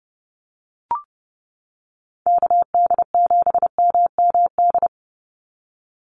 Morse code, heard as a keyed tone, about 30 wpm, 700 Hz. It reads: KD7MMD
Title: USA Call Signs 30wpm